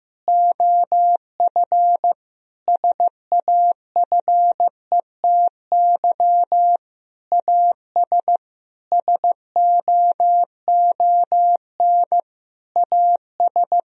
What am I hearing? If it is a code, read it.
Morse code, heard as a keyed tone, about 15 wpm, 700 Hz. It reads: OF SAFETY AS SOON AS
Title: Sentences from Top 1000 Words - Review Part 4 - 15wpm